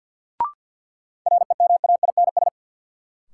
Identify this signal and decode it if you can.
Morse code, heard as a keyed tone, about 46 wpm, 700 Hz. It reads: REMAINS
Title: Sets of 5 Words Encore - New Word Drill - 50wpm